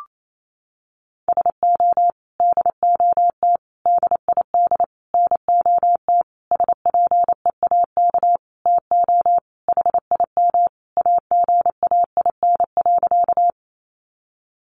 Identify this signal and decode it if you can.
Morse code, heard as a keyed tone, about 28 wpm, 700 Hz. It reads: SO DOT DID NOT SPEAK TO HIM AGAIN.
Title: Sentences from Top 1500 Words 28wpm